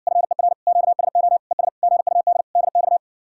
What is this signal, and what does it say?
Morse code, heard as a keyed tone, about 61 wpm, 700 Hz. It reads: FER QSO ES GUD DX